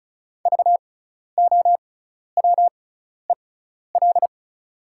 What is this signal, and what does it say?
Morse code, heard as a keyed tone, about 34 wpm, spaced as for 12 wpm, 700 Hz. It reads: VOWEL
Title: 6x Character Spacing - Top 500 Words - 35wpm